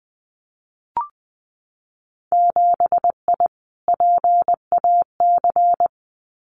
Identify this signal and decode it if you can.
Morse code, heard as a keyed tone, about 20 wpm, 700 Hz. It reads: 7I PAC
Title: Winter Field Day Exchanges Encore 20wpm